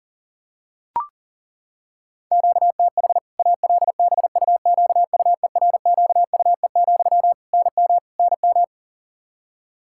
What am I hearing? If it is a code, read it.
Morse code, heard as a keyed tone, about 40 wpm, 700 Hz. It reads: QTH ALBUQUERQUE, NM NM
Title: K1USN SST QSO Practice 40wpm